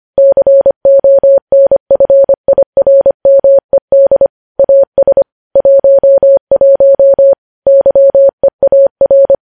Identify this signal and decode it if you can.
Morse code, heard as a keyed tone, about 25 wpm, 550 Hz. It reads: CONFIRMED AS 11 YEAR
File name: podcast / News headlines-1x-25x25